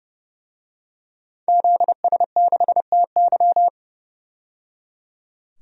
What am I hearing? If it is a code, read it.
Morse code, heard as a keyed tone, about 29 wpm, 690 Hz. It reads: ZS6TY
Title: DXCC-100 Call Signs - 5 Characters - 30wpm